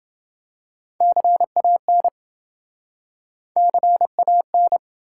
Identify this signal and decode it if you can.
Morse code, heard as a keyed tone, about 28 wpm, 700 Hz. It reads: CAN CAN